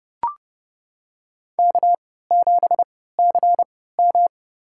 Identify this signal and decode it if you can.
Morse code, heard as a keyed tone, about 30 wpm, spaced as for 17 wpm, 700 Hz. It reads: K7CM
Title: USA Call Signs - 4 Characters - 3x Character Spacing - 30wpm